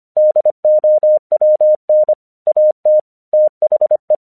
Morse code, heard as a keyed tone, about 26 wpm, 610 Hz. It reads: DOWN AT THE
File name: 1 English 25wpm / AQC36